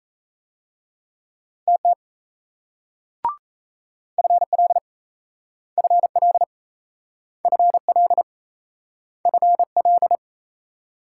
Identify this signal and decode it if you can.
Morse code, heard as a keyed tone, about 39 wpm, 700 Hz. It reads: TT FL FL FL FL